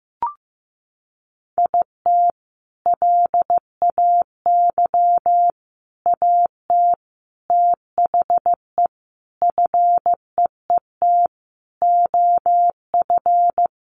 Morse code, heard as a keyed tone, about 15 wpm, 700 Hz. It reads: IT LAY AT THE FEET OF
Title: Sentences from Top 1500 Words 15wpm